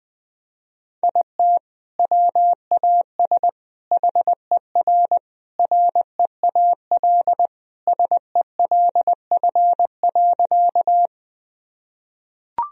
Morse code, heard as a keyed tone, about 20 wpm, 700 Hz. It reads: IT WAS HER REAL SELF.
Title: Sentences from Top 1000 Words - Review Part 1 - 20wpm